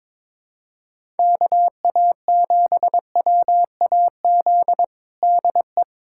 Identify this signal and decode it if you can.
Morse code, heard as a keyed tone, about 21 wpm, 700 Hz. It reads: KA7WAZ DE
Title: USA Callsign DE Callsign - 6 Characters - 22wpm